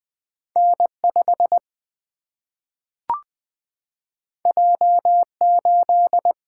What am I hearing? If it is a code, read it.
Morse code, heard as a keyed tone, about 20 wpm, 700 Hz. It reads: N 5 J 8
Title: Single-Letter Single-Number 20wpm